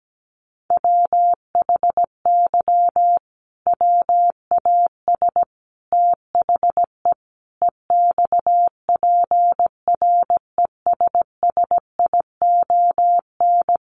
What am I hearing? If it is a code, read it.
Morse code, heard as a keyed tone, about 17 wpm, 700 Hz. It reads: WHY WAS THE EXPRESSION
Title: Sentences from Top 1400 Words 17wpm